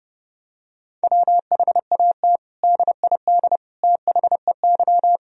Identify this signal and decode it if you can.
Morse code, heard as a keyed tone, about 32 wpm, 700 Hz. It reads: WHAT DID THEY
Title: Sentences from Top 1000 Words - Review Part 4 - 30wpm